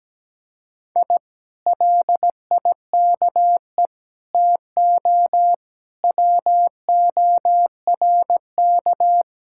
Morse code, heard as a keyed tone, about 17 wpm, 700 Hz. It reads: I LIKE TO WORK
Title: Speed-Racing Common QSO Phrases - 17wpm